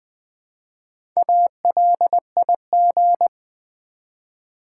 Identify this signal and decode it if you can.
Morse code, heard as a keyed tone, about 20 wpm, 700 Hz. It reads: ALIG
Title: Build A Word - 1 Letter At A Time - 3 to 5-Letter Words - 20wpm